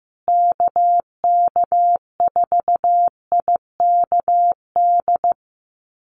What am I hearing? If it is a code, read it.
Morse code, heard as a keyed tone, about 15 wpm, 700 Hz. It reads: KK4IKD
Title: USA Call Signs Repeated 2x 15wpm